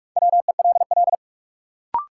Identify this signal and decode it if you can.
Morse code, heard as a keyed tone, about 46 wpm, 700 Hz. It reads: WELL
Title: Sets of 2 Words - New Word Drill - 45wpm